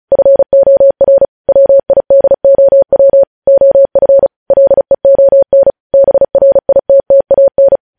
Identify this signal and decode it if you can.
Morse code, heard as a keyed tone, about 36 wpm, 550 Hz. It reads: FOR WIDOW OF LEON BRITTAN
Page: news headlines in morse code at 35 wpm